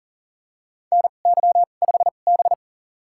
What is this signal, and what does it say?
Morse code, heard as a keyed tone, about 37 wpm, spaced as for 28 wpm, 700 Hz. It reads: NY5B
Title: USA Call Signs - 4 Characters - 2x Character Spacing - 40wpm